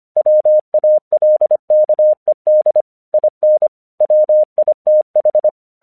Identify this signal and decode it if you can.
Morse code, heard as a keyed tone, about 25 wpm, 610 Hz. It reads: WALKED IN WITH
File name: 1 English 25wpm / AQC02